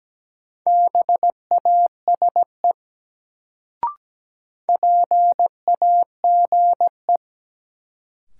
Morse code, heard as a keyed tone, about 17 wpm, 700 Hz. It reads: BASE PAGE
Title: Sentences from Top 1000 Words - Review Part 1 - New Word Drill - 17wpm